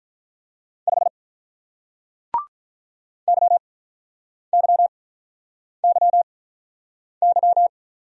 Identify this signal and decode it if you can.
Morse code, heard as a keyed tone, about 48 wpm, 700 Hz. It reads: 5 Y Y Y Y